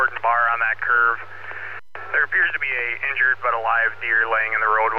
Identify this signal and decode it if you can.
Emergency Alert System (SAME) tones, heard as harsh buzzing bursts, with speech. It is heard from 1.2 to 4.3 s.